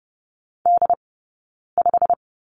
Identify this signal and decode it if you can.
Morse code, heard as a keyed tone, about 30 wpm, 700 Hz. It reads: D 5
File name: course / 30:10wpm / 086 Lesson 15d - Number 5 - Two Characters 30wpm